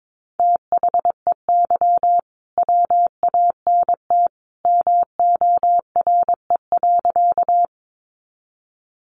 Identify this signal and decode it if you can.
Morse code, heard as a keyed tone, about 22 wpm, 700 Hz. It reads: THEY WANT MORE.
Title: Sentences from Top 200 Words 22wpm